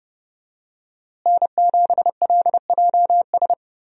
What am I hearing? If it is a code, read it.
Morse code, heard as a keyed tone, about 30 wpm, 700 Hz. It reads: N7LJS
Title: USA Call Signs - 5 Characters - Repeated 2x - 30wpm